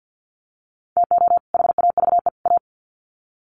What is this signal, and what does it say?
Morse code, heard as a keyed tone, about 50 wpm, 700 Hz. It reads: TO HAVE A
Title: Sets of 3 Words 50wpm